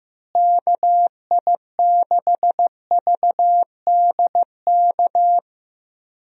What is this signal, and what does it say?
Morse code, heard as a keyed tone, about 15 wpm, 700 Hz. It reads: KI6VDK
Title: USA Call Signs - 6 Characters - Repeated 2x - 15wpm